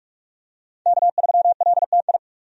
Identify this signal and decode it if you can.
Morse code, heard as a keyed tone, about 47 wpm, 700 Hz. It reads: K3LTI
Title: USA Call Signs - 5 Characters - 45wpm